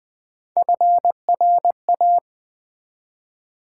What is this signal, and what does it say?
Morse code, heard as a keyed tone, about 20 wpm, 700 Hz. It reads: FRA
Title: Build A Word - 1 Letter At A Time - 3 to 5-Letter Words - 20wpm